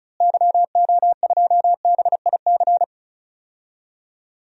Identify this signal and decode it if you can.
Morse code, heard as a keyed tone, about 36 wpm, 700 Hz. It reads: YO2BIC
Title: DXCC-100 Call Signs - 6 Characters - Repeated 3x - 35wpm